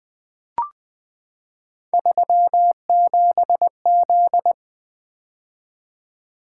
Morse code, heard as a keyed tone, about 20 wpm, 700 Hz. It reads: 37Z
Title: Three Letter-Number Combinations - 20wpm